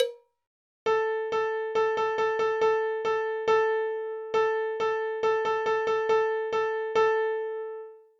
There is no code.